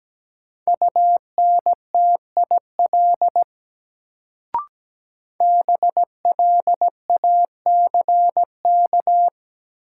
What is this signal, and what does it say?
Morse code, heard as a keyed tone, about 17 wpm, 700 Hz. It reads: UNTIL BLACK
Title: Sets of 5 Words Encore - New Word Drill - 17wpm